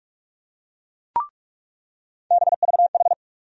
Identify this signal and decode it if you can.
Morse code, heard as a keyed tone, about 46 wpm, 700 Hz. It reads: BVH
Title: USA Call Sign Postfixes - 3 Letters 45wpm